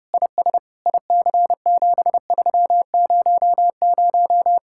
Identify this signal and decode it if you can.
Morse code, heard as a keyed tone, about 30 wpm, 700 Hz. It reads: IS IC7300